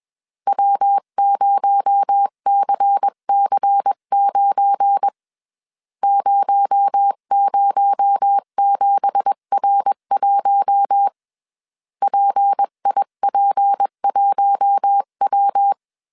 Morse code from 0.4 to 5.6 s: W0CC9